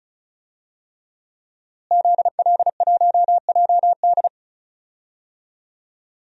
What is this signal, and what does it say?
Morse code, heard as a keyed tone, about 35 wpm, 700 Hz. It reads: ZL1JD
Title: DXCC-100 Call Signs - 5 Characters - Repeated 3x - 35wpm